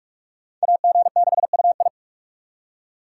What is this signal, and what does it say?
Morse code, heard as a keyed tone, about 44 wpm, 700 Hz. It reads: AG6UI